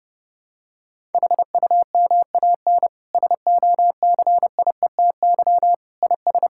Morse code, heard as a keyed tone, about 30 wpm, 700 Hz. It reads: HUMAN SOCIETY IS
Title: Sentences from Top 1000 Words - Review Part 6 - 30wpm